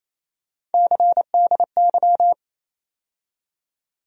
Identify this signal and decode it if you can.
Morse code, heard as a keyed tone, about 28 wpm, 700 Hz. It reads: CDY
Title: USA Call Sign Postfixes - 3 Letters 28wpm